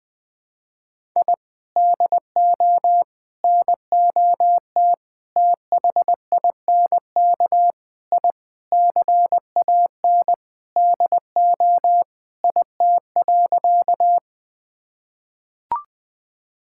Morse code, heard as a keyed tone, about 20 wpm, 700 Hz. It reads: I DO NOT THINK I CAN DO IT.
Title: Sentences with 5-Word Phrases - 20wpm